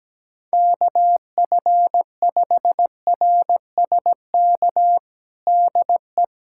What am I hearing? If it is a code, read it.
Morse code, heard as a keyed tone, about 17 wpm, 700 Hz. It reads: KF5RSK DE
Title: USA Callsign DE Callsign - 6 Characters - 17wpm